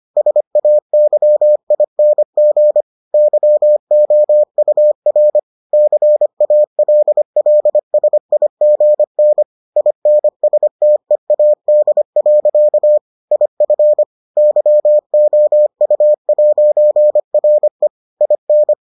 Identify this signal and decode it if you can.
Morse code, heard as a keyed tone, about 25 wpm, 600 Hz. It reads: SAYING YOUR CALLSIGN INSTEAD. IF YOU'RE IN